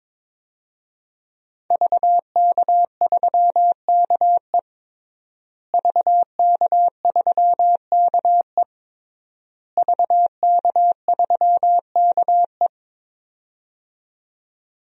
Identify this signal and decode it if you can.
Morse code, heard as a keyed tone, about 22 wpm, 700 Hz. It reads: VK3KE VK3KE VK3KE
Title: DXCC-100 Call Signs - 5 Characters - Repeated 3x - 22wpm